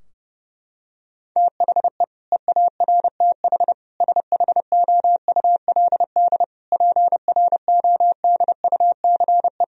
Morse code, heard as a keyed tone, about 30 wpm, 700 Hz. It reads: THE EARTH SHOULD PRODUCE
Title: Sentences from Top 1000 Words - Review Part 3 - 30wpm